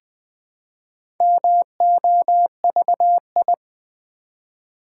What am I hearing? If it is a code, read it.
Morse code, heard as a keyed tone, about 20 wpm, 700 Hz. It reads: MOVI
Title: Build A Word - 1 Letter At A Time - 3 to 5-Letter Words - 20wpm